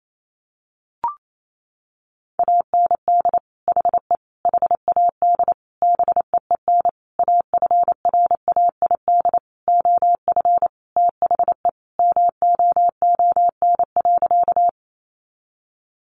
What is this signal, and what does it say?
Morse code, heard as a keyed tone, about 28 wpm, 700 Hz. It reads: AND HE HAD BEEN AFRAID OF THE MOON.
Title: Sentences from Top 1500 Words 28wpm